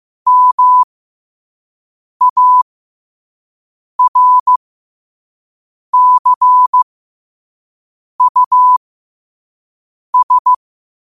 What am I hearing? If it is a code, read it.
Morse code, heard as a keyed tone, about 15 wpm, spaced as for 5 wpm, 1000 Hz. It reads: MARCUS
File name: qotdfiles / 5wpm